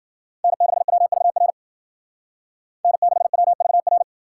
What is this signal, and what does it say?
Morse code, heard as a keyed tone, about 52 wpm, 700 Hz. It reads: N6WVR N6WVR